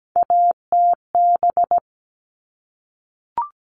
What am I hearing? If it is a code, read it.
Morse code, heard as a keyed tone, about 17 wpm, 700 Hz. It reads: ATB
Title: All Cut Numbers - 3-Digit 17wpm